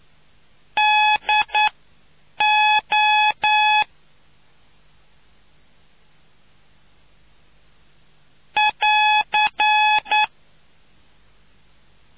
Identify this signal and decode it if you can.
Morse code, heard as a keyed tone, about 9 wpm, 850 Hz. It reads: D O <AR>